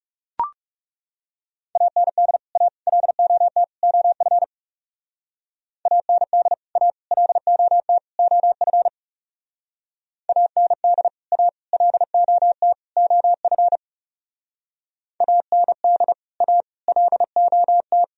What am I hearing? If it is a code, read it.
Morse code, heard as a keyed tone, about 40 wpm, 700 Hz. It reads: AND A LOT OF AND A LOT OF AND A LOT OF AND A LOT